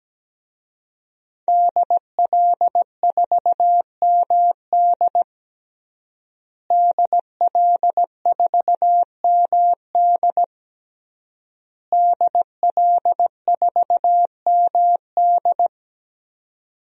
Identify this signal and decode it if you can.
Morse code, heard as a keyed tone, about 17 wpm, 700 Hz. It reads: DL4MD DL4MD DL4MD
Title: DXCC-100 Call Signs - 5 Characters - Repeated 3x - 17wpm